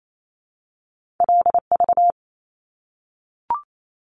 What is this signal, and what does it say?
Morse code, heard as a keyed tone, about 28 wpm, 700 Hz. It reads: LV
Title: Top Two Letter Combinations in Words 28wpm